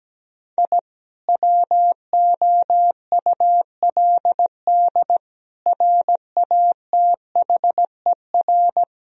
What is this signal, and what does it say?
Morse code, heard as a keyed tone, about 17 wpm, 700 Hz. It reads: I WOULD RATHER
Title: Sentences from Top 1000 Words - Review Part 4 - 17wpm